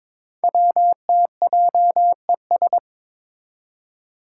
Morse code, heard as a keyed tone, about 22 wpm, 700 Hz. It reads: WTJES
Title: Code Groups - 5 Letters - 22wpm